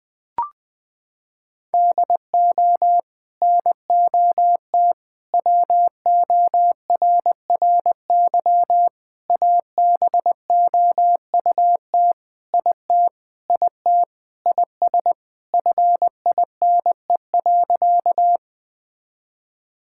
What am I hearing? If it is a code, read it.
Morse code, heard as a keyed tone, about 19 wpm, 700 Hz. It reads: DO NOT WORRY ABOUT IT IT IS FINE.
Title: Sentences with 5-Word Phrases - 20wpm